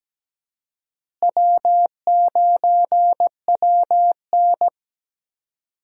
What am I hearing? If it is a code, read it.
Morse code, heard as a keyed tone, about 17 wpm, 700 Hz. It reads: W9WN
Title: USA Call Signs - 4 Characters - Repeated 3x - 17wpm